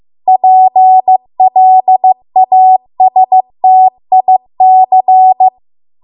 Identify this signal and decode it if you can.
Morse code, heard as a keyed tone, about 15 wpm, 750 Hz. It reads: PLASTIC